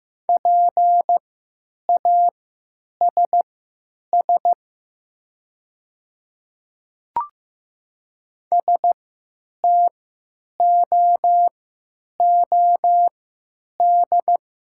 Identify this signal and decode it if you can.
Morse code, heard as a keyed tone, about 15 wpm, spaced as for 9 wpm, 700 Hz. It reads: PASS STOOD
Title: 3x Character Spacing - Top 500 Words - 15wpm